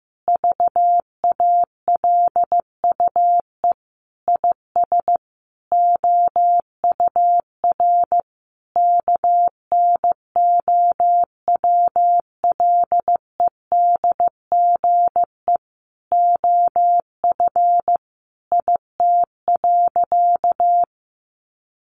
Morse code, heard as a keyed tone, about 15 wpm, 700 Hz. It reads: VALUE IS OUR KNOWLEDGE OF IT.
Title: Sentences from Top 1400 Words 15wpm